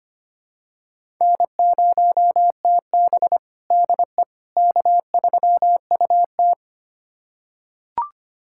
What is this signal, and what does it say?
Morse code, heard as a keyed tone, about 25 wpm, 700 Hz. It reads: N0TB DE K3UT
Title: USA Callsign DE Callsign - 4 Characters - 25wpm